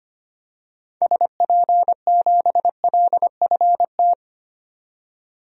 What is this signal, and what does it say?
Morse code, heard as a keyed tone, about 25 wpm, 690 Hz. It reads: SP7LFT
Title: DXCC-100 Call Signs - 6 Characters - 25wpm